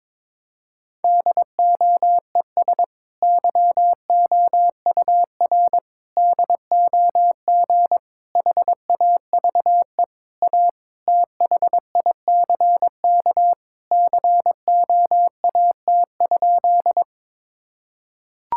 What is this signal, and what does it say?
Morse code, heard as a keyed tone, about 22 wpm, 700 Hz. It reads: DOES YOUR DOG HAVE A THICK COAT?
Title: Sentences from Top 1600 Words 22wpm